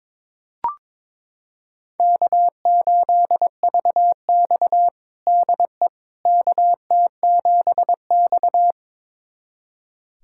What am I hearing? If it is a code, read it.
Morse code, heard as a keyed tone, about 22 wpm, 700 Hz. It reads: K8VX DE KT7X
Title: USA Callsign DE Callsign - 4 Characters - 22wpm